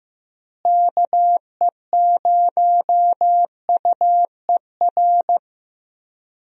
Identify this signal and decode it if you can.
Morse code, heard as a keyed tone, about 15 wpm, 700 Hz. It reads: KE0UER
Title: USA Call Signs - 6 Characters - Repeated 3x - 15wpm